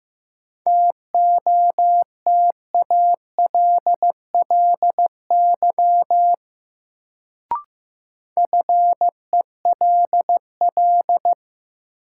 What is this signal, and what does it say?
Morse code, heard as a keyed tone, about 15 wpm, 700 Hz. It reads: TOTALLY FELL
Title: Sentences from Top 1600 Words - New Word Drill - 15wpm